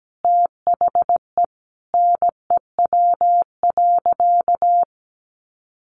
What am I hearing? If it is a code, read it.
Morse code, heard as a keyed tone, about 17 wpm, 700 Hz. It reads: THE NEW.
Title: Sentences from Top 1300 Words 17wpm